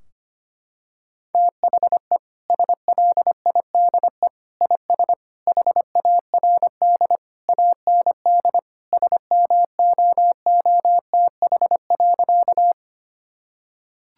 Morse code, heard as a keyed tone, about 25 wpm, 700 Hz. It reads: THE SLIDE IS HARD AND SMOOTH.
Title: Sentences from Top 1600 Words 25wpm